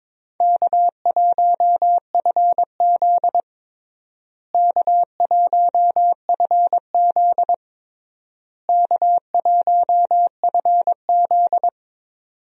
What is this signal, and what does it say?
Morse code, heard as a keyed tone, about 22 wpm, 700 Hz. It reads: K1FZ K1FZ K1FZ
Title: USA Call Signs - 4 Characters - Repeated 3x - 22wpm